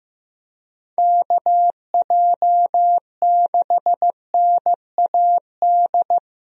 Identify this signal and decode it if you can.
Morse code, heard as a keyed tone, about 15 wpm, 700 Hz. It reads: KJ6NAD
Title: USA Call Signs - 6 Characters - 15wpm